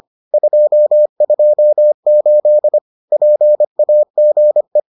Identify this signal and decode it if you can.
Morse code, heard as a keyed tone, about 25 wpm, 600 Hz. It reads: 228 PAGE